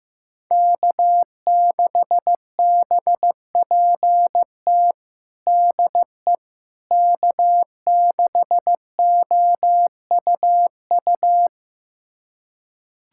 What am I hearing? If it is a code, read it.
Morse code, heard as a keyed tone, about 15 wpm, 700 Hz. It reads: K6BPT DE K6OUU
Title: USA Callsign DE Callsign - 5 Characters - 15wpm